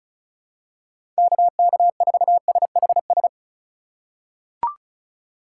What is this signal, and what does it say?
Morse code, heard as a keyed tone, about 36 wpm, 700 Hz. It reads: KK4SHS
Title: USA Call Signs - 6 Characters - 3x Character Spacing - 35wpm